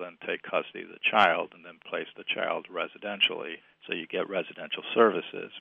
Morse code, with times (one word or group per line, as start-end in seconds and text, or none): none